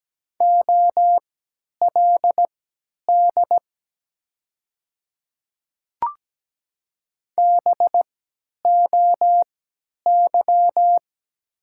Morse code, heard as a keyed tone, about 17 wpm, spaced as for 10 wpm, 700 Hz. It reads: OLD BOY
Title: 3x Character Spacing - Top 200 Words - 17wpm